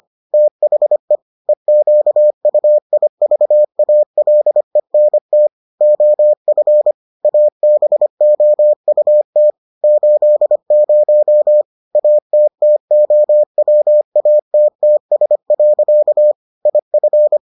Morse code, heard as a keyed tone, about 25 wpm, 600 Hz. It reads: THE EQUIVALENT OF ABOUT 80 ATTOWATTS. IF